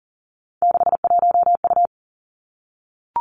Morse code, heard as a keyed tone, about 38 wpm, 700 Hz. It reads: 61U
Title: Three Challenging Letter-Number Combinations 40wpm